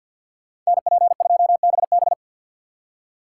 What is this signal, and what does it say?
Morse code, heard as a keyed tone, about 59 wpm, 700 Hz. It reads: NP2BB
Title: Speed Racing - USA 2x2 Call Signs 50wpm